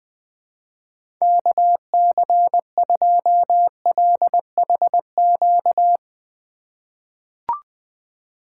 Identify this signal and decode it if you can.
Morse code, heard as a keyed tone, about 20 wpm, 700 Hz. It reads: KC2LHQ
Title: USA Call Signs - 6 Characters - 3x Character Spacing - 20wpm